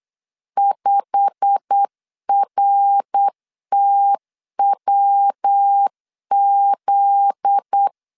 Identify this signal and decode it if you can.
Morse code, heard as a keyed tone, about 8 wpm, 800 Hz. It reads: 5RTWZ
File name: 8wpm